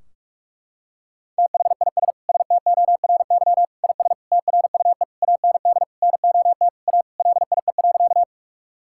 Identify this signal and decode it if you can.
Morse code, heard as a keyed tone, about 46 wpm, 700 Hz. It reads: THIS STORY IS TRUE AND NOT A LIE.
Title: Sentences from Top 1600 Words 45wpm